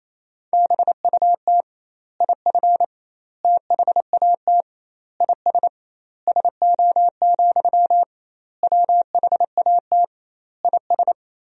Morse code, heard as a keyed tone, about 28 wpm, 700 Hz. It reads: BUT IF THAT IS SO, WHAT IS
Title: 2x Word-Spacing - Sentences from Top 200 Words - 28wpm